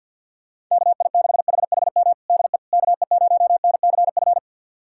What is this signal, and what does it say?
Morse code, heard as a keyed tone, about 52 wpm, 700 Hz. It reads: KI6HHM DE KE0NCF
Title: USA Callsign DE Callsign 50wpm